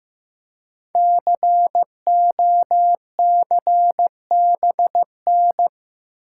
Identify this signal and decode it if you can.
Morse code, heard as a keyed tone, about 15 wpm, 700 Hz. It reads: COCBN
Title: Code Groups - 5 Letters - 15wpm